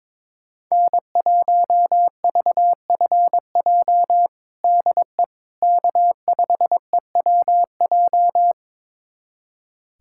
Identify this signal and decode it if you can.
Morse code, heard as a keyed tone, about 22 wpm, 700 Hz. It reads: N1VFJ DE K5EWJ
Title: USA Callsign DE Callsign - 5 Characters - 22wpm